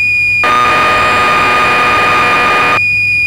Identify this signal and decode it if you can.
Cassette data (Kansas City type).
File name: magazines / Acorn User / 1984-09 BBC Programs